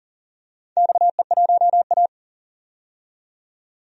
Morse code, heard as a keyed tone, about 40 wpm, 700 Hz. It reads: XE1A
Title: DXCC-100 Call Signs - 4 Characters - 40wpm